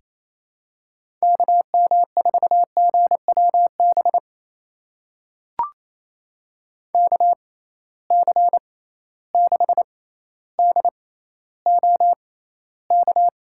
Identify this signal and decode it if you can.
Morse code, heard as a keyed tone, about 28 wpm, spaced as for 14 wpm, 700 Hz. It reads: KM4GWB KC6DOK
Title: USA Call Signs - 6 Characters - 6x Character Spacing - 28wpm